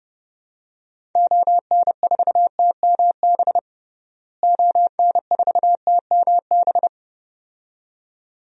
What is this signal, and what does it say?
Morse code, heard as a keyed tone, about 30 wpm, 700 Hz. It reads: ON4TMB ON4TMB
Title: DXCC-100 Call Signs - 6 Characters - Repeated 2x - 30wpm